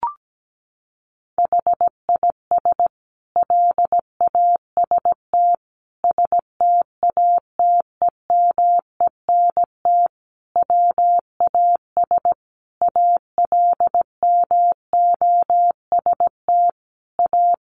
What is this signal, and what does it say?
Morse code, heard as a keyed tone, about 17 wpm, 700 Hz. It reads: HIS LAST STATEMENT WAS ALMOST A
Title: Sentences from Top 1200 Words 17wpm